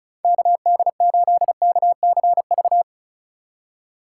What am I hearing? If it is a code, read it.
Morse code, heard as a keyed tone, about 34 wpm, 700 Hz. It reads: KD8KCV
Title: USA Call Signs - 6 Characters - 2x Character Spacing - 35wpm